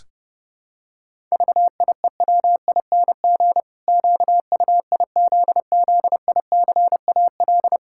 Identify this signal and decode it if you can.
Morse code, heard as a keyed tone, about 30 wpm, 700 Hz. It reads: VIEWING QUIZZICAL
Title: Pangram Challenge v2 - 30wpm